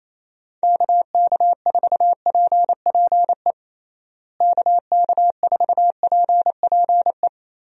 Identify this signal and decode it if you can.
Morse code, heard as a keyed tone, about 28 wpm, 700 Hz. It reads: KK4PPE KK4PPE
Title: USA Call Signs - 6 Characters - Repeated 3x - 28wpm